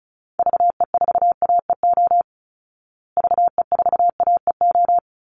Morse code, heard as a keyed tone, about 34 wpm, 700 Hz. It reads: VE4AEO VE4AEO
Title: Call Signs Repeated 2x - 35wpm